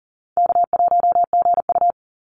Morse code, heard as a keyed tone, about 40 wpm, 700 Hz. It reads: K1GU
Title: Easy Call Signs Repeated 2x - 40wpm